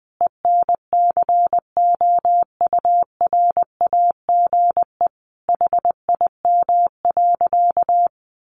Morse code, heard as a keyed tone, about 20 wpm, 700 Hz. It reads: ENCOURAGE HIM.
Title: Sentences from Top 1400 Words 20wpm